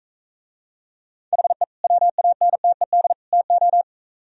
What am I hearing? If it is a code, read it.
Morse code, heard as a keyed tone, about 45 wpm, 700 Hz. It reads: HE WANTED TO